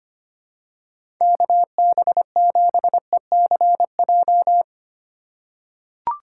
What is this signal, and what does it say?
Morse code, heard as a keyed tone, about 25 wpm, 700 Hz. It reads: KB7ECJ
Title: USA Call Signs - 6 Characters - 2x Character Spacing - 25wpm